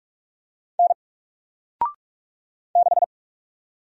Morse code, heard as a keyed tone, about 44 wpm, 700 Hz. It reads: N 6
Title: Single Letter-Number 45wpm